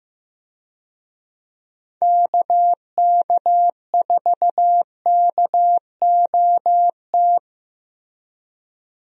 Morse code, heard as a keyed tone, about 15 wpm, 700 Hz. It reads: KK4KOT
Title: USA Call Signs - 6 Characters - 15wpm